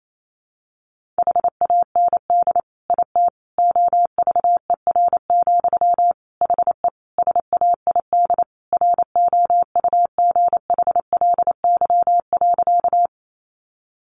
Morse code, heard as a keyed tone, about 28 wpm, 700 Hz. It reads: HAND IT OVER, HE SAID ROUGHLY.
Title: Sentences from Top 1500 Words 28wpm